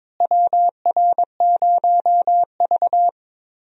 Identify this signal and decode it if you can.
Morse code, heard as a keyed tone, about 22 wpm, 700 Hz. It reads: WR0V
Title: USA Call Signs - 4 Characters - Repeated 2x - 22wpm